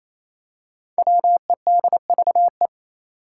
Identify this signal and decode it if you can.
Morse code, heard as a keyed tone, about 28 wpm, 700 Hz. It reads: WEDVE
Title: Five Challenging Letter-Number Combinations 28wpm